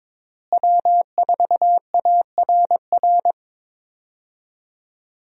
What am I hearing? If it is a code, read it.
Morse code, heard as a keyed tone, about 22 wpm, 700 Hz. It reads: W4ARR
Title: USA Call Signs - 5 Characters - Repeated 3x - 22wpm